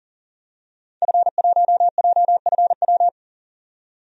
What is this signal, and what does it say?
Morse code, heard as a keyed tone, about 40 wpm, 700 Hz. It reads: F1JFW
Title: DXCC-100 Call Signs - 5 Characters - 40wpm